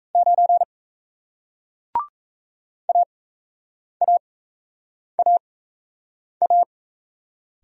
Morse code, heard as a keyed tone, about 38 wpm, spaced as for 8 wpm, 700 Hz. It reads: 9 AAAA